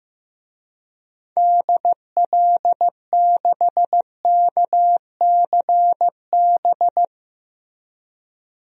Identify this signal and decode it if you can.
Morse code, heard as a keyed tone, about 15 wpm, 700 Hz. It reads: DL6KCB
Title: DXCC-100 Call Signs - 6 Characters - Repeated 3x - 15wpm